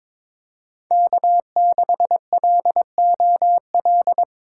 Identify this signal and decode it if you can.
Morse code, heard as a keyed tone, about 22 wpm, 700 Hz. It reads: K6LOL
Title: USA Call Signs - 5 Characters - 8x Character Spacing - 22wpm